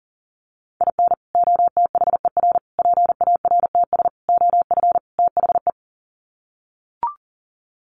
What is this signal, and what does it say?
Morse code, heard as a keyed tone, about 40 wpm, 700 Hz. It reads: IN OTHER PARTS OF THE
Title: Sets of 5 Words 40wpm